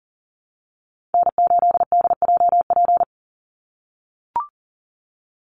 Morse code, heard as a keyed tone, about 40 wpm, 700 Hz. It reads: N8DJP
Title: USA Call Signs 40wpm